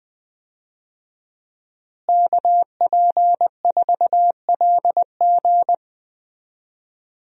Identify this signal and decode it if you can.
Morse code, heard as a keyed tone, about 20 wpm, 700 Hz. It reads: KP4LG
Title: DXCC-100 Call Signs - 5 Characters - Repeated 2x - 20wpm